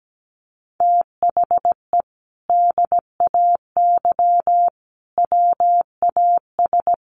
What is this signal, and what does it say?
Morse code, heard as a keyed tone, about 17 wpm, 700 Hz. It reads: THE DAY WAS
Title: Sentences from Top 1300 Words 17wpm